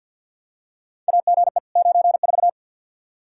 Morse code, heard as a keyed tone, about 52 wpm, 700 Hz. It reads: AGE 94
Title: Practice QSO - Exchange 3 - More Info - 50wpm